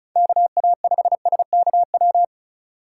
Morse code, heard as a keyed tone, about 36 wpm, 700 Hz. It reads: KA5SKW
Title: USA Call Signs - 6 Characters - 4x Character Spacing - 35wpm